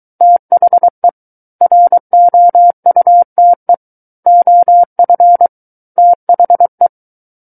Morse code, heard as a keyed tone, about 23 wpm, 700 Hz. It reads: THE ROUTE OF THE
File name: Book One / 14-Chapter 14 - 23 WPM